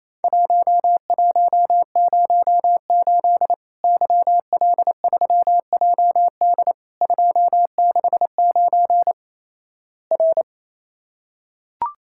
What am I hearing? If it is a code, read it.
Morse code, heard as a keyed tone, about 27 wpm, 700 Hz. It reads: 1108 YL3JD 269 R
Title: Worked All Europe DX Contest - QTC Exchange - 28wpm